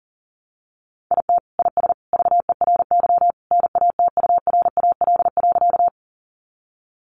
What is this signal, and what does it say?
Morse code, heard as a keyed tone, about 40 wpm, 700 Hz. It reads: IT IS VERY NATURAL.
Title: Sentences from Top 700 Words 40wpm